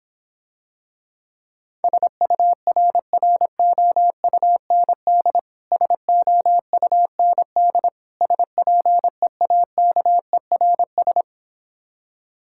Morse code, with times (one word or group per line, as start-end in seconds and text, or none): 1.8-5.4 SURROUND
5.7-7.9 SOUND
8.2-11.2 SPEAKERS